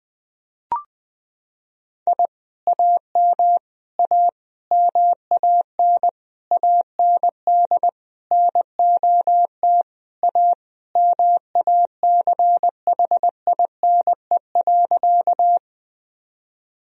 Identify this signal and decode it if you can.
Morse code, heard as a keyed tone, about 21 wpm, 700 Hz. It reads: I AM A MAN AND NOT A MACHINE.
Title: Sentences from Top 1000 Words - Review Part 2 - 20wpm